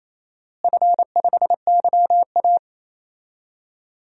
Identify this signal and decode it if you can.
Morse code, heard as a keyed tone, about 28 wpm, 700 Hz. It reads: F5YA
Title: DXCC-100 Call Signs - 4 Characters - Repeated 2x - 28wpm